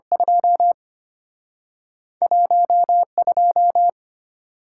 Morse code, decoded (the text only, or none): 2 12